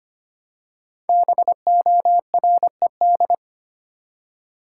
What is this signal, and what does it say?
Morse code, heard as a keyed tone, about 25 wpm, 700 Hz. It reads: BORED
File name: 5 Letter Words - First 3 Letters Given - 25wpm